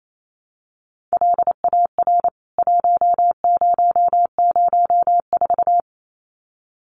Morse code, heard as a keyed tone, about 27 wpm, 700 Hz. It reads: LAR 1004
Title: CWT Exchange 28wpm